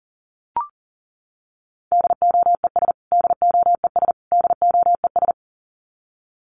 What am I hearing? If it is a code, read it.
Morse code, heard as a keyed tone, about 40 wpm, 700 Hz. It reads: DOES DOES DOES
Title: Top 500 Words Repeated 40wpm